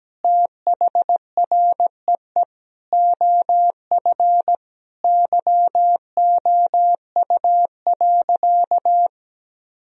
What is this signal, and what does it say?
Morse code, heard as a keyed tone, about 17 wpm, 700 Hz. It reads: THREE OF YOU.
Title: Sentences from Top 1000 Words - Review Part 3 - 17wpm